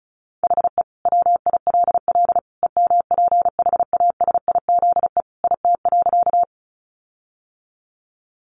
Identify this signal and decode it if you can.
Morse code, heard as a keyed tone, about 34 wpm, 700 Hz. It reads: HE WILL EMPHASIZE IT.
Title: Sentences from Top 1500 Words 35wpm